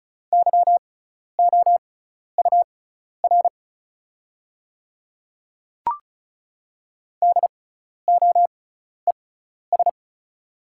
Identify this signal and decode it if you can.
Morse code, heard as a keyed tone, about 36 wpm, spaced as for 12 wpm, 700 Hz. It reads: YOUR DOES
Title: 6x Character Spacing - Top 200 Words - 35wpm